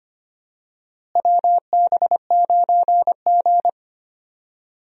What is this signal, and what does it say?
Morse code, heard as a keyed tone, about 25 wpm, 700 Hz. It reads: WB9G
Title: USA Call Signs - 4 Characters - Repeated 2x - 25wpm